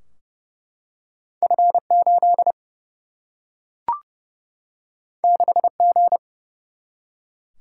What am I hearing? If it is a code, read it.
Morse code, heard as a keyed tone, about 30 wpm, 700 Hz. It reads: F8 6G
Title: Single-Letter Single-Number 30wpm